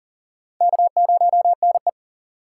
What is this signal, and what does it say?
Morse code, heard as a keyed tone, about 39 wpm, 700 Hz. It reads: K0NE
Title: USA Call Signs - 4 Characters - 6x Character Spacing - 40wpm